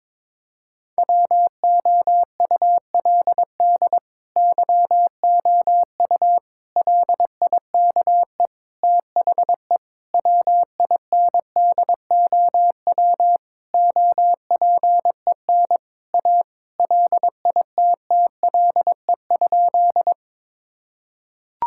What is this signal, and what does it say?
Morse code, heard as a keyed tone, about 22 wpm, 700 Hz. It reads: WOULD YOU LIKE THE WINDOW OPEN A LITTLE?
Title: Sentences from Top 1600 Words 22wpm